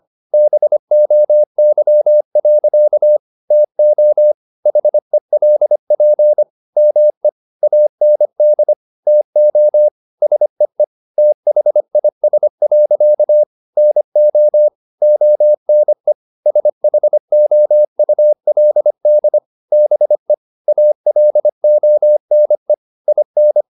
Morse code, heard as a keyed tone, about 25 wpm, 600 Hz. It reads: BOY. TO HELP ME AND TO SEE THIS. NO ONE SHOULD BE ALONE IN
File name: the old man and the sea part 1 25wpm